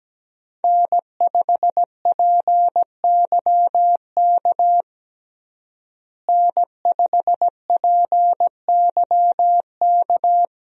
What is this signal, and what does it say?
Morse code, heard as a keyed tone, about 17 wpm, 700 Hz. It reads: N5PYK N5PYK
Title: USA Call Signs - 5 Characters - Repeated 2x - 17wpm